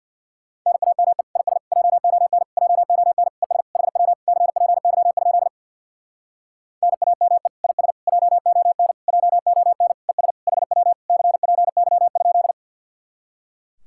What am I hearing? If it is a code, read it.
Morse code, heard as a keyed tone, about 64 wpm, 700 Hz. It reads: NAME IS JON JON ES HW CPY? NAME IS JON JON ES HW CPY?